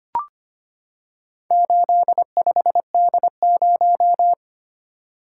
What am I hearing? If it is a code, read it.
Morse code, heard as a keyed tone, about 25 wpm, 700 Hz. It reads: 85D0
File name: practice / 25wpm / Four Letter-Number Combinations - 25wpm